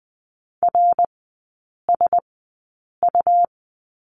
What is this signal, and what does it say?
Morse code, heard as a keyed tone, about 20 wpm, spaced as for 8 wpm, 700 Hz. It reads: RSU
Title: 138 Lesson 23e - Letter f - Three Characters 20wpm